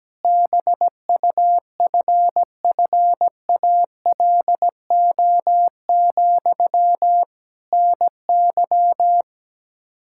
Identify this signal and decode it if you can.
Morse code, heard as a keyed tone, about 17 wpm, 700 Hz. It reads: BUFFALO, NY
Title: Practice QSO - Exchange 1 - Beginning - 17wpm